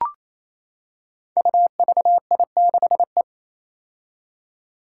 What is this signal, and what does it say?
Morse code, heard as a keyed tone, about 28 wpm, 700 Hz. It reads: UVI6E
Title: Code Groups - 5 Letter:Numbers - 28wpm